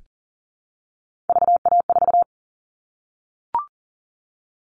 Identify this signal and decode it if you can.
Morse code, heard as a keyed tone, about 40 wpm, 700 Hz. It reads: VA4
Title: All Cut Numbers - 3-Digit 40wpm